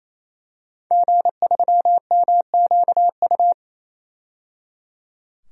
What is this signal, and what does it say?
Morse code, heard as a keyed tone, about 28 wpm, 700 Hz. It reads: G3MQU
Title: DXCC-100 Call Signs - 5 Characters - 28wpm